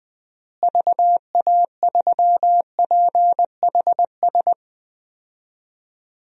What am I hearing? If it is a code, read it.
Morse code, heard as a keyed tone, about 21 wpm, 700 Hz. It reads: VA3PHS
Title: DXCC-100 Call Signs - 6 Characters - Repeated 3x - 20wpm